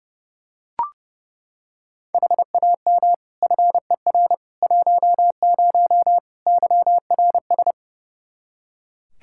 Morse code, heard as a keyed tone, about 30 wpm, 700 Hz. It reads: HAM FER 10 YRS
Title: K1USN SST QSO Practice 30wpm